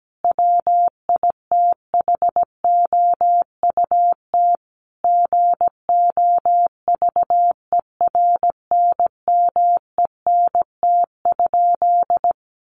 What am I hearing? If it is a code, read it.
Morse code, heard as a keyed tone, about 17 wpm, 700 Hz. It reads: WITHOUT GOVERNMENT?
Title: Sentences from Top 600 Words 17wpm